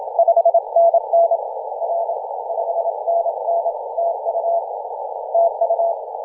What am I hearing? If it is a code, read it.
Morse code, heard as a keyed tone, about 27 wpm, 690 Hz. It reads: 5NN R 5NN TU TU
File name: 2016/Sep/25 0323z A35JP 18.085MHz CW